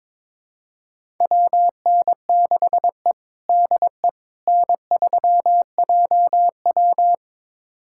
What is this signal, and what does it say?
Morse code, heard as a keyed tone, about 21 wpm, 700 Hz. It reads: WN6E DE N3JW
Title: USA Callsign DE Callsign - 4 Characters - 22wpm